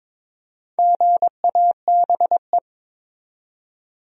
Morse code, heard as a keyed tone, about 22 wpm, 700 Hz. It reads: GABE
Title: USA Common Names - Max 4 Letters - 22wpm